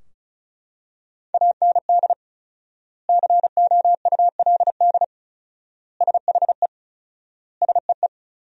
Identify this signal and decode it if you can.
Morse code, heard as a keyed tone, about 34 wpm, 700 Hz. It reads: AND COULD SHE SEE
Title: 4x Word-Spacing - Sentences from Top 100 Words - 35wpm